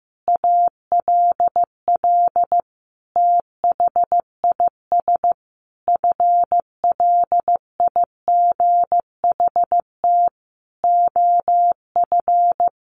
Morse code, heard as a keyed tone, about 15 wpm, 700 Hz. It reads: ALL THIS FLIGHT OF
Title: Sentences from Top 1300 Words 15wpm